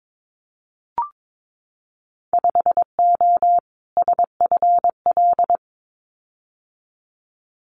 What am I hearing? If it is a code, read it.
Morse code, heard as a keyed tone, about 22 wpm, 700 Hz. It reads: 5O SFL
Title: Winter Field Day Exchanges 22wpm